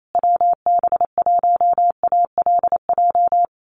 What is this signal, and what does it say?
Morse code, heard as a keyed tone, about 28 wpm, 700 Hz. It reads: WB1ALJ
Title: USA Call Signs Repeated 2x 28wpm